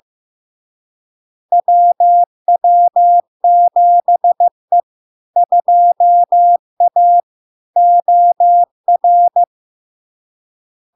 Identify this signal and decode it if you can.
Morse code, heard as a keyed tone, about 15 wpm, 700 Hz. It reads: WW7E 2A OR
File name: LARK-FD-2022-Log-in-CW-15wpm-700-Hz